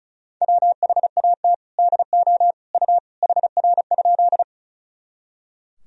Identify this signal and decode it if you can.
Morse code, heard as a keyed tone, about 36 wpm, 700 Hz. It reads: WHAT DO U HR?
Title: Sentences from Top 300 Words and Abbreviations 35wpm